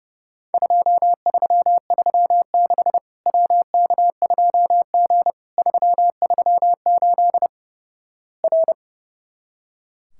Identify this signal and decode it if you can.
Morse code, heard as a keyed tone, about 30 wpm, 700 Hz. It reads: 2336 WK2G 338 R
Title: Worked All Europe DX Contest - QTC Exchange - 30wpm